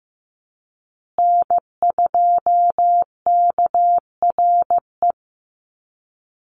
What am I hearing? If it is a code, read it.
Morse code, heard as a keyed tone, about 15 wpm, 700 Hz. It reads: N2KRE
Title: USA Call Signs Repeated 2x 15wpm